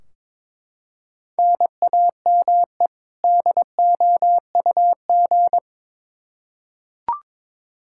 Morse code, heard as a keyed tone, about 21 wpm, 700 Hz. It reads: NAME DOUG
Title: Practice QSO - Exchange 1 - Beginning - 22wpm